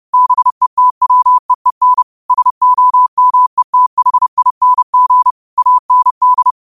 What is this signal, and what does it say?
Morse code, heard as a keyed tone, about 30 wpm, 1000 Hz. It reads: BETWEEN SOMETHING AND
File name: qotdfiles / 30wpm